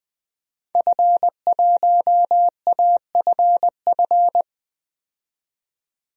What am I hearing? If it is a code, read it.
Morse code, heard as a keyed tone, about 20 wpm, 700 Hz. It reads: F1AFF
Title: DXCC-100 Call Signs - 5 Characters - 20wpm